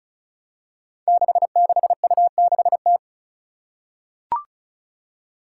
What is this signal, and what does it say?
Morse code, heard as a keyed tone, about 36 wpm, 700 Hz. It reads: 66U6T
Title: Five Challenging Letter-Number Combinations 35wpm